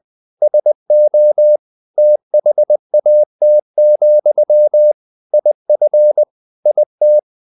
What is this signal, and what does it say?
Morse code, heard as a keyed tone, about 20 wpm, 600 Hz. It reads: SO THAT, IF IT
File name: the old man and the sea / the old man and the sea part 1 20wpm